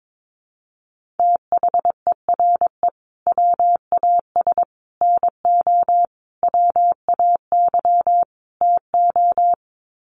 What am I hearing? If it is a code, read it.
Morse code, heard as a keyed tone, about 22 wpm, 700 Hz. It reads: THERE WAS NO WAY TO
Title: Sets of 5 Words Encore 22wpm